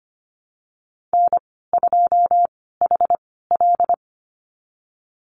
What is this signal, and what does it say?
Morse code, heard as a keyed tone, about 25 wpm, spaced as for 16 wpm, 700 Hz. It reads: N2HL
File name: Lesson 15g - Number 5 - Call Signs 25wpm